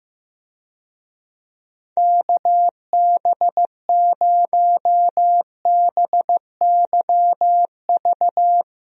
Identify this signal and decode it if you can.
Morse code, heard as a keyed tone, about 15 wpm, 700 Hz. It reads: KB0BYV
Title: USA Call Signs - 6 Characters - Repeated 3x - 15wpm